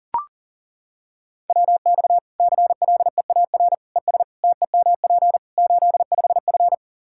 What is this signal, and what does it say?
Morse code, heard as a keyed tone, about 37 wpm, 700 Hz. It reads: WX CLEAR ES TEMP 85F
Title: Practice QSO - Exchange 2 - Info - 40wpm